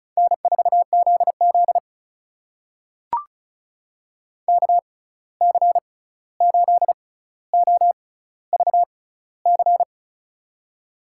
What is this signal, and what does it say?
Morse code, heard as a keyed tone, about 36 wpm, spaced as for 16 wpm, 700 Hz. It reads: N4ZZ KC8OVC
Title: USA Call Signs - 6x Character Spacing - 35wpm